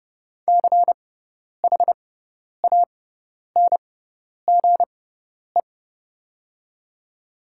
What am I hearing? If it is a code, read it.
Morse code, heard as a keyed tone, about 30 wpm, spaced as for 10 wpm, 700 Hz. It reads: CHANGE
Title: 6x Character Spacing - Top 200 Words - 30wpm